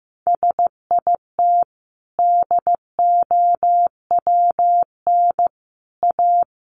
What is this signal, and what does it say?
Morse code, heard as a keyed tone, about 15 wpm, 700 Hz. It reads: SIT DOWN A
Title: Sentences from Top 1000 Words 15wpm